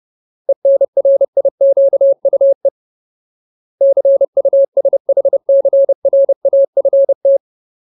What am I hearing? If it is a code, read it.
Morse code, heard as a keyed tone, about 30 wpm, 550 Hz. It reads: ENRIQUE CUSHCRAFT
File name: Ultimate Word List / 30WPM Ultimate